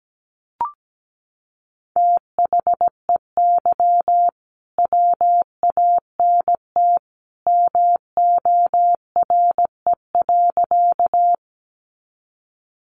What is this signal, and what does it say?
Morse code, heard as a keyed tone, about 17 wpm, 700 Hz. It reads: THEY WANT MORE.
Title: Sentences from Top 200 Words 17wpm